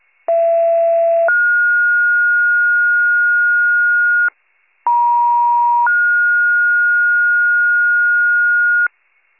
Two-tone page, heard as a one-second tone, then a three-second tone, about 670 then 1430 Hz, about 950 then 1430 Hz, with noise. It is heard from 0.3 to 8.9 s.